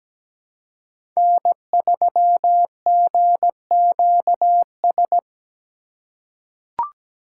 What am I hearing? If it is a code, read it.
Morse code, heard as a keyed tone, about 17 wpm, 700 Hz. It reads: N3GQS
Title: USA Call Signs - 5 Characters - 4x Character Spacing - 17wpm